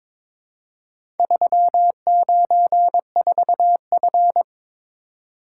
Morse code, heard as a keyed tone, about 22 wpm, 700 Hz. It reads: 394F
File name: practice / 22wpm / Four Letter-Number Combinations - 22wpm